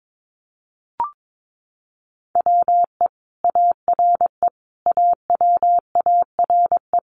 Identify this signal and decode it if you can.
Morse code, heard as a keyed tone, about 22 wpm, 700 Hz. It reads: WE ARE AWARE
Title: Sentences from Top 1200 Words 22wpm